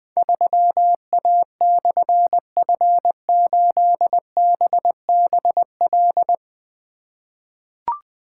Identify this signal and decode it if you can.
Morse code, heard as a keyed tone, about 20 wpm, 700 Hz. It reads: 3A/F8BBL
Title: Difficult Call Signs 20wpm